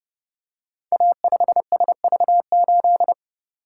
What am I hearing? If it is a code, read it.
Morse code, heard as a keyed tone, about 30 wpm, 700 Hz. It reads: A5SV8
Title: Code Groups - 5 Letter:Numbers - 30wpm